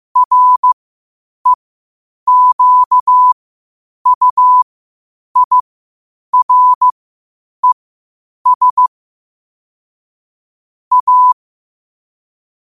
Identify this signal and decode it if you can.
Morse code, heard as a keyed tone, about 15 wpm, spaced as for 8 wpm, 1000 Hz. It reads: REQUIRES A
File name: qotdfiles / 8wpm